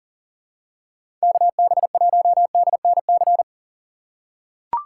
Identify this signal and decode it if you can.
Morse code, heard as a keyed tone, about 38 wpm, 700 Hz. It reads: KB1DNC
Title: USA Call Signs - 6 Characters - 3x Character Spacing - 40wpm